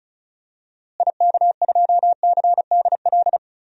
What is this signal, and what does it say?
Morse code, heard as a keyed tone, about 36 wpm, 700 Hz. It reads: IK2CDL
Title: DXCC-100 Call Signs - 6 Characters - Repeated 3x - 35wpm